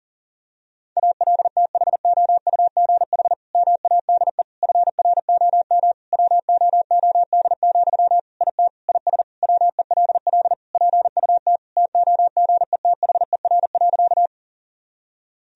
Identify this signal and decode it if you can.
Morse code, heard as a keyed tone, about 40 wpm, 700 Hz. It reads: ALTHOUGH MADE FROM WOOD, IT IS WELL PUT TOGETHER.
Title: Sentences from Top 1000 Words - Review Part 3 - 40wpm